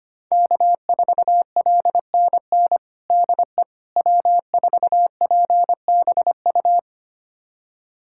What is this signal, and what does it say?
Morse code, heard as a keyed tone, about 25 wpm, 700 Hz. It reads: K4LNN DE W4PBU
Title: USA Callsign DE Callsign - 5 Characters - 25wpm